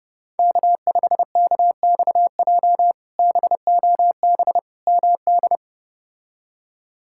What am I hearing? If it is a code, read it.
Morse code, heard as a keyed tone, about 31 wpm, 700 Hz. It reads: K5KXJ BOB MD
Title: K1USN SST QSO Practice - 30wpm